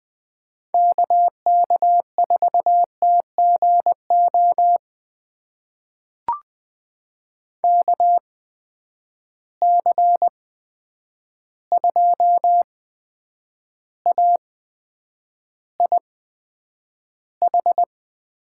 Morse code, heard as a keyed tone, about 20 wpm, spaced as for 9 wpm, 700 Hz. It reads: KK4TGO KC2AIH
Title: USA Call Signs - 6 Characters - 8x Character Spacing - 20wpm